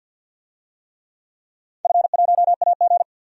Morse code, heard as a keyed tone, about 52 wpm, 700 Hz. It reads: F1AG
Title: DXCC-100 Call Signs - 4 Characters - 50wpm